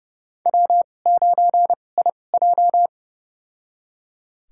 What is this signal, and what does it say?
Morse code, heard as a keyed tone, about 30 wpm, spaced as for 22 wpm, 700 Hz. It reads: W9IJ
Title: USA Call Signs - 4 Characters - 2x Character Spacing - 30wpm